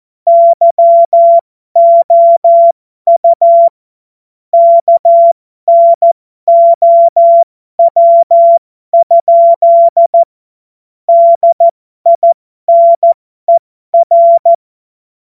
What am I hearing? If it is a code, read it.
Morse code, heard as a keyed tone, about 14 wpm, spaced as for 12 wpm, 680 Hz. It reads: YOU KNOW? DINER
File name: Morsekurs / Witze